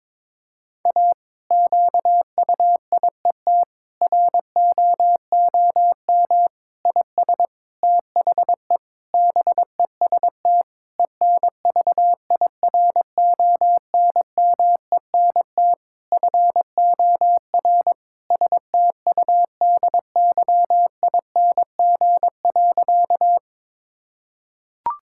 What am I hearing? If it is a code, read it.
Morse code, heard as a keyed tone, about 21 wpm, 700 Hz. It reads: A QUIET ROOM IS THE BEST ENVIRONMENT FOR STUDYING.
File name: Difficult Sentences with Hints - 22wpm